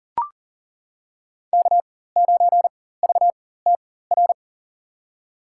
Morse code, heard as a keyed tone, about 40 wpm, spaced as for 19 wpm, 700 Hz. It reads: K9VTR
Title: USA Call Signs - 5 Characters - 4x Character Spacing - 40wpm